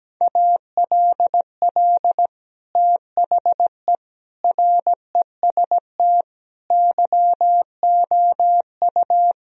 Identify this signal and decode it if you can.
Morse code, heard as a keyed tone, about 17 wpm, 700 Hz. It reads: ALL THE REST YOU
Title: Sentences from Top 1000 Words - Review Part 4 - 17wpm